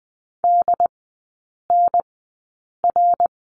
Morse code, heard as a keyed tone, about 20 wpm, spaced as for 8 wpm, 700 Hz. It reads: DNR
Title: 205 Lesson 33e - Letter j - Three Characters 20wpm